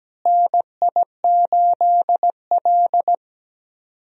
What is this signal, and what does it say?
Morse code, heard as a keyed tone, about 17 wpm, 700 Hz. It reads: NI8L
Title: USA Call Signs - 4 Characters - Repeated 3x - 17wpm